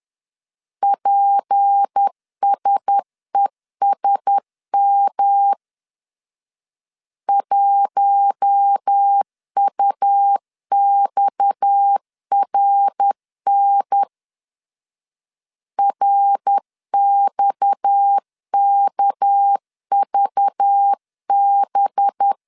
Morse code, heard as a keyed tone, about 10 wpm, 800 Hz. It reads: PSESM 1UXRN RXKVB
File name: LEZIONI / 6 PKX0 / 6 PKX 10wpm 2426